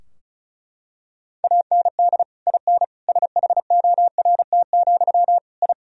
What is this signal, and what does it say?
Morse code, heard as a keyed tone, about 36 wpm, 700 Hz. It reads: AND IN SHORT, I
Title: Sentences from Top 1000 Words - Review Part 2 - 35wpm